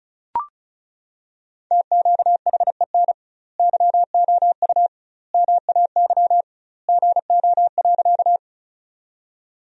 Morse code, heard as a keyed tone, about 36 wpm, 700 Hz. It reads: TQHEN YOU MAY GO.
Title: 2x Word-Spacing - Sentences from Top 100 Words - 35wpm